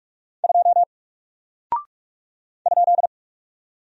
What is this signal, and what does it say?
Morse code, heard as a keyed tone, about 45 wpm, 700 Hz. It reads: 2 ?
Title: Single Character (Letter:Number:Punctuation:Prosign) Alt 45wpm